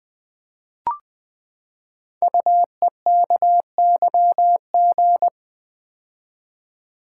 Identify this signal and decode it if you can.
Morse code, heard as a keyed tone, about 20 wpm, 700 Hz. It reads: UEKYG
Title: Code Groups - 5 Letters - 20wpm